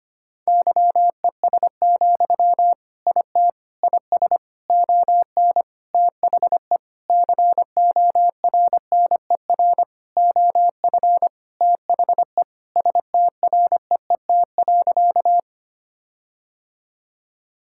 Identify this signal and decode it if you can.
Morse code, heard as a keyed tone, about 25 wpm, 700 Hz. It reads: YES, IT IS ON THE CORNER OF THE STREET.
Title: Sentences from Top 1600 Words 25wpm